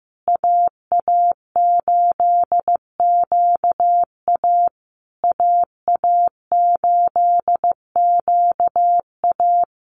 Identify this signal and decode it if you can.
Morse code, heard as a keyed tone, about 15 wpm, 700 Hz. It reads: AA8QA AA8QA
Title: USA Call Signs Repeated 3x 15wpm